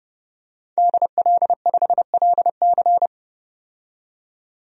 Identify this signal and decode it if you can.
Morse code, heard as a keyed tone, about 31 wpm, 690 Hz. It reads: DL5LC
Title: DXCC-100 Call Signs - 5 Characters - 30wpm